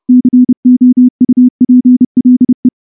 Morse code, heard as a keyed tone, about 28 wpm, 250 Hz. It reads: COUPLE
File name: morse / morse-training-30WPM